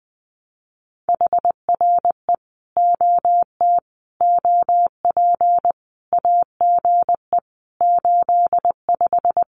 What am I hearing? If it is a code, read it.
Morse code, heard as a keyed tone, about 20 wpm, 700 Hz. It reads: HRE OT OP AGE 85
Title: Common QSO Phrases 20wpm